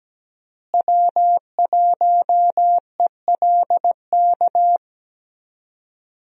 Morse code, heard as a keyed tone, about 17 wpm, 700 Hz. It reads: W1ELK
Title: USA Call Signs - 5 Characters - Repeated 3x - 17wpm